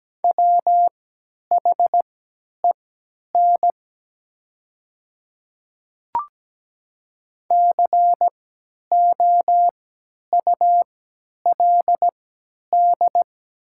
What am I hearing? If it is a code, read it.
Morse code, heard as a keyed tone, about 17 wpm, spaced as for 10 wpm, 700 Hz. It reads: WHEN COULD
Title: 3x Character Spacing - Top 100 Words - 17wpm